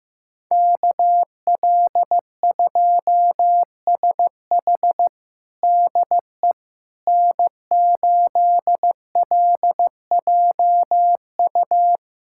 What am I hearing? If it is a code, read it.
Morse code, heard as a keyed tone, about 16 wpm, 700 Hz. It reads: KL2SH DE N8LJU
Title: USA Callsign DE Callsign 15wpm